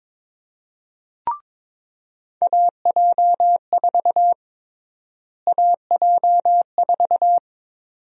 Morse code, heard as a keyed tone, about 22 wpm, 700 Hz. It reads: AJ4 AJ4
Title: USA Call Sign Prefixes - Repeated 2x - 22wpm